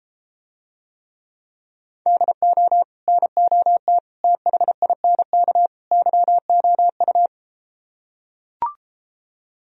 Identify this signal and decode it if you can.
Morse code, heard as a keyed tone, about 34 wpm, 700 Hz. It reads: DO NOT THINK YOU